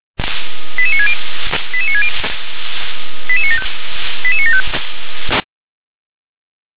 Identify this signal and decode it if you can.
Five-tone selective call (ZVEI), heard as quick runs of five tones, tones of about 70 ms. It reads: 88960 88960 88975 88975
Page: /357/SDR/ZVEI/